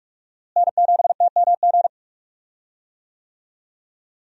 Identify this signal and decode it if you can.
Morse code, heard as a keyed tone, about 46 wpm, 700 Hz. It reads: N7TMG
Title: USA Call Signs - 5 Characters - Repeated 3x - 45wpm